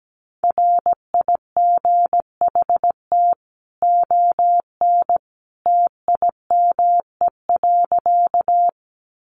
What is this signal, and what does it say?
Morse code, heard as a keyed tone, about 17 wpm, 700 Hz. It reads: RIGHT ON TIME.
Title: A Sound Of Thunder 17wpm